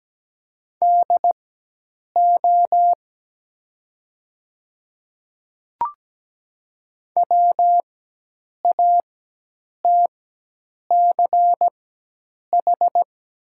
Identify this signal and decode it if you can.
Morse code, heard as a keyed tone, about 17 wpm, spaced as for 8 wpm, 700 Hz. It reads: DO WATCH
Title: 4x Character Spacing - Top 300 Words - 17wpm